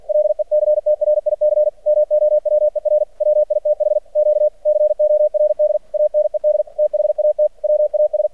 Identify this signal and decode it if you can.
Morse code, heard as a keyed tone, about 50 wpm, 600 Hz. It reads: LECTRIC MOWER WITH Q CORD ANED THAT WAS
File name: archived / CRSnet